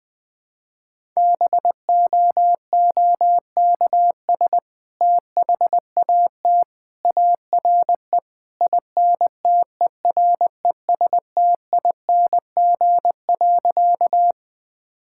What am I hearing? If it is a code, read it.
Morse code, heard as a keyed tone, about 20 wpm, 700 Hz. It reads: BOOKS THAT ARE INTERESTING.
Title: Sentences with 2-Word Phrases - 20wpm